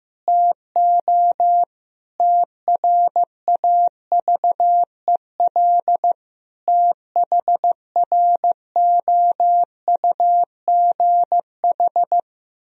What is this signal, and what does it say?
Morse code, heard as a keyed tone, about 15 wpm, 700 Hz. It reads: TO TRAVEL THROUGH
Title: Sentences from Top 1000 Words - Review Part 2 - 15wpm